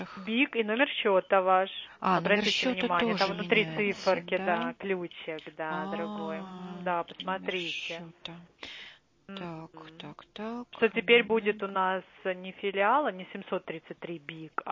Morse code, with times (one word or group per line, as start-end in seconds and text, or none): none